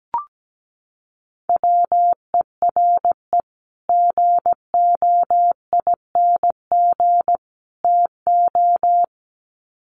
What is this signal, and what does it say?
Morse code, heard as a keyed tone, about 17 wpm, 700 Hz. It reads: WERE GOING TO
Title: Sets of 3 Words 17wpm